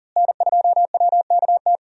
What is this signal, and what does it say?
Morse code, heard as a keyed tone, about 40 wpm, 700 Hz. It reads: N2WKT
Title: USA Call Signs - 5 Characters - Repeated 3x - 40wpm